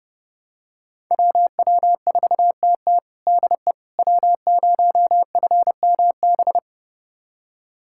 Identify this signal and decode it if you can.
Morse code, heard as a keyed tone, about 30 wpm, 700 Hz. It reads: WW4TT DE W0FMB
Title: USA Callsign DE Callsign - 5 Characters - 30wpm